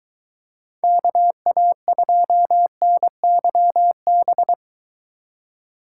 Morse code, heard as a keyed tone, about 23 wpm, 700 Hz. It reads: KA2NYB